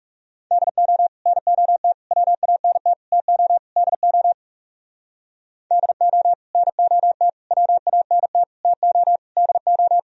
Morse code, heard as a keyed tone, about 37 wpm, 700 Hz. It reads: DO NOT WANT TO DO DO NOT WANT TO DO
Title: Speed-Racing - Sets of 5 Words 30wpm